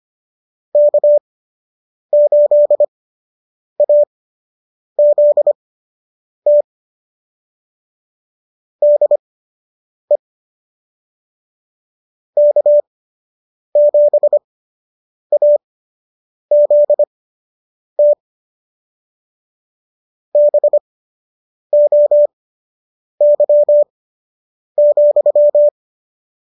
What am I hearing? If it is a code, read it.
Morse code, heard as a keyed tone, about 25 wpm, spaced as for 8 wpm, 590 Hz. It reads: K8AZT DE K7AZT BOY,
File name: qso006 08